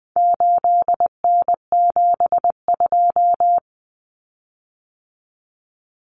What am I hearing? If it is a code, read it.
Morse code, heard as a keyed tone, about 20 wpm, 700 Hz. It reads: 8N72
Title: Common Cut Numbers - 4-Digit 20wpm